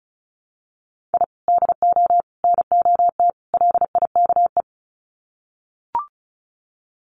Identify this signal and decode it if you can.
Morse code, heard as a keyed tone, about 34 wpm, 700 Hz. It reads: I DO NOT LIKE
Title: Sets of 4 Words 35wpm